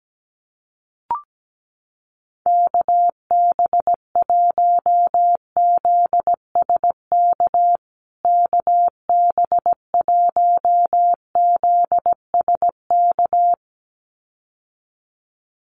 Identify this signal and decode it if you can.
Morse code, heard as a keyed tone, about 17 wpm, 700 Hz. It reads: KB1ZSK KB1ZSK
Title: USA Call Signs Repeated 2x 17wpm